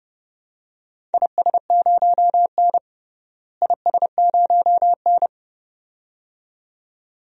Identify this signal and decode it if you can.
Morse code, heard as a keyed tone, about 30 wpm, 700 Hz. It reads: IS0N IS0N
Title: DXCC-100 Call Signs - 4 Characters - Repeated 2x - 30wpm